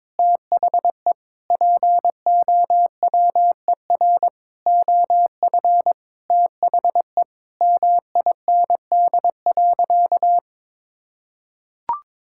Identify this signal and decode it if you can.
Morse code, heard as a keyed tone, about 22 wpm, 700 Hz. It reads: THE POWER OF THE MIND.
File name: practice / 22wpm / Sentences with 4-Word Phrases - 22wpm